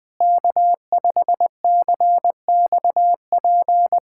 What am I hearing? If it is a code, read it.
Morse code, heard as a keyed tone, about 20 wpm, 700 Hz. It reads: K5CXP
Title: USA Callsign DE Callsign - 5 Characters - 20wpm